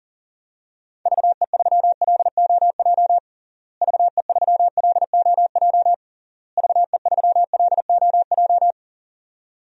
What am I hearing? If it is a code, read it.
Morse code, heard as a keyed tone, about 40 wpm, 700 Hz. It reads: VE3LOJ VE3LOJ VE3LOJ
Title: DXCC-100 Call Signs - 6 Characters - Repeated 3x - 40wpm